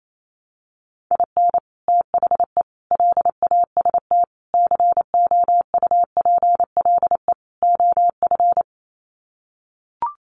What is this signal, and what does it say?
Morse code, heard as a keyed tone, about 28 wpm, 700 Hz. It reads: IN THE LAST COUPLE OF
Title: Sets of 5 Words 28wpm